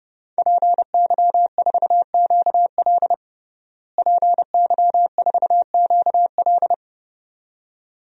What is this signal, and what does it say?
Morse code, heard as a keyed tone, about 30 wpm, 700 Hz. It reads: PY4QL PY4QL
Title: DXCC-100 Call Signs - 5 Characters - Repeated 2x - 30wpm